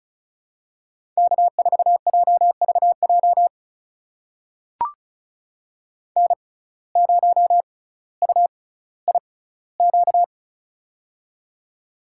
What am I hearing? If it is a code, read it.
Morse code, heard as a keyed tone, about 34 wpm, spaced as for 18 wpm, 700 Hz. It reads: K4JVJ N0UIQ
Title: USA Call Signs - 5 Characters - 6x Character Spacing - 35wpm